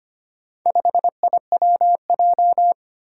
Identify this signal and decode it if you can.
Morse code, heard as a keyed tone, about 25 wpm, 700 Hz. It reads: 5IWJ
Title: Four Challenging Letter-Number Combinations 25wpm